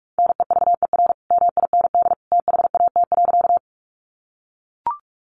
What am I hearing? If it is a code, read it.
Morse code, heard as a keyed tone, about 44 wpm, 700 Hz. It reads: NEVER MIND THAT.
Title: Sentences from Top 500 Words 45wpm